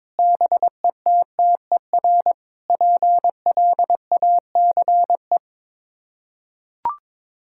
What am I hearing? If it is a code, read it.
Morse code, heard as a keyed tone, about 22 wpm, 700 Hz. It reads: BETTER PLACE
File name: practice / 22wpm / Sets of 7 Words - 22wpm